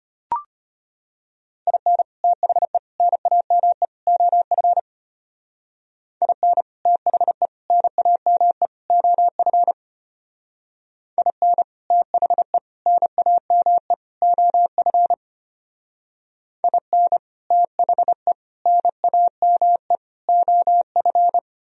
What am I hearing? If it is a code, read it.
Morse code, heard as a keyed tone, about 32 wpm, 700 Hz. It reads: IN THE NAME OF IN THE NAME OF IN THE NAME OF IN THE NAME OF